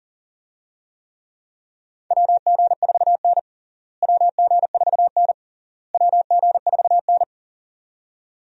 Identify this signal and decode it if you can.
Morse code, heard as a keyed tone, about 40 wpm, 700 Hz. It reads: WG4N WG4N WG4N
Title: USA Call Signs - 4 Characters - Repeated 3x - 40wpm